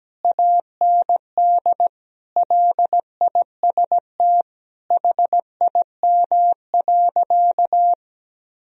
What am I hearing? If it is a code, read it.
Morse code, heard as a keyed tone, about 17 wpm, 700 Hz. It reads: AND LIST HIM.
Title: Sentences from Top 1000 Words - Review Part 2 - 17wpm